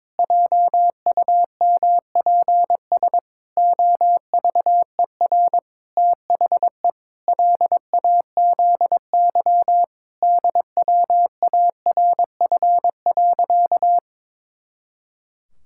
Morse code, heard as a keyed tone, about 21 wpm, 700 Hz. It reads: JUMPS OVER THE LAZY DWARF.
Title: Pangram Challenge v2 - 22wpm